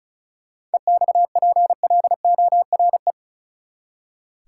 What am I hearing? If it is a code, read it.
Morse code, heard as a keyed tone, about 36 wpm, 700 Hz. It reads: EXPLORE
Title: Seven Letter Words - 35wpm